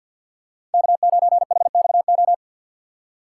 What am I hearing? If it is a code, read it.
Morse code, heard as a keyed tone, about 50 wpm, 700 Hz. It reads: K9HXO
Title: USA Call Signs - 6x Character Spacing - 50wpm